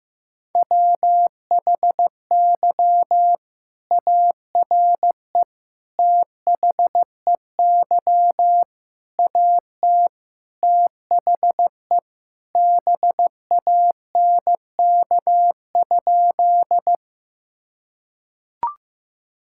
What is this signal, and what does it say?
Morse code, heard as a keyed tone, about 15 wpm, 700 Hz. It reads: WHY ARE THEY AT THE BANK?
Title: Sentences from Top 1000 Words - Review Part 6 - 15wpm